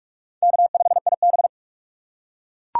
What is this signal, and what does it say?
Morse code, heard as a keyed tone, about 42 wpm, 690 Hz. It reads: K5IB
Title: USA Call Signs - 4 Characters - 3x Character Spacing - 45wpm